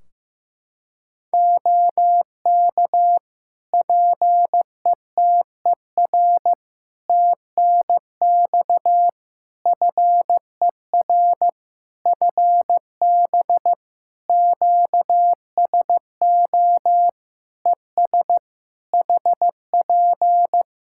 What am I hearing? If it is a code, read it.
Morse code, heard as a keyed tone, about 15 wpm, 700 Hz. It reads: OK PETER TNX FER FB QSO ES HP
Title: K1USN SST QSO Practice 15wpm